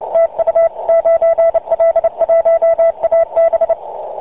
Morse code, heard as a keyed tone, about 29 wpm, 660 Hz, with radio noise. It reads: TU 9L1AB
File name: 9l1ab-qso